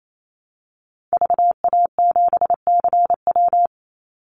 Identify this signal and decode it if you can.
Morse code, heard as a keyed tone, about 28 wpm, 700 Hz. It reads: VA7CW
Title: Call Signs - 28wpm